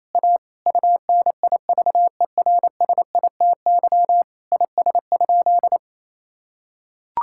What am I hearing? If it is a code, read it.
Morse code, heard as a keyed tone, about 28 wpm, 700 Hz. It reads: A UNIVERSITY IS?
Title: Sentences from Top 1000 Words - Review Part 5 - 28wpm